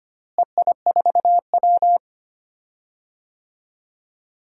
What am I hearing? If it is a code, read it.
Morse code, heard as a keyed tone, about 25 wpm, 700 Hz. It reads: EI4W
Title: DXCC-100 Call Signs - 4 Characters - Repeated 3x - 25wpm